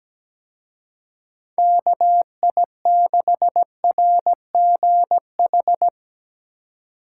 Morse code, heard as a keyed tone, about 18 wpm, 700 Hz. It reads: KI6RGH
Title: Speed Racing - USA 2x3 Call Signs 17wpm